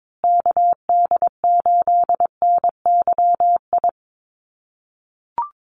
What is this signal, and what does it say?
Morse code, heard as a keyed tone, about 22 wpm, 700 Hz. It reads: KD8NYI
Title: USA Call Signs Repeated 3x 22wpm